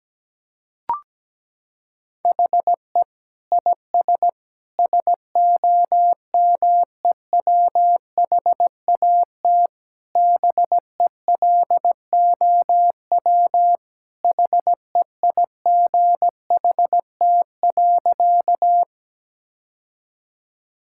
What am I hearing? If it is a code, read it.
Morse code, heard as a keyed tone, about 17 wpm, 700 Hz. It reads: HE IS SOMEWHAT BELOW HEIGHT.
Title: Sentences from Top 1600 Words 17wpm